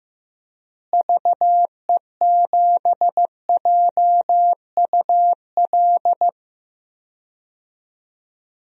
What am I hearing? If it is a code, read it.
Morse code, heard as a keyed tone, about 15 wpm, 700 Hz. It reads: VE7JUL
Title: DXCC-100 Call Signs - 6 Characters - Repeated 3x - 15wpm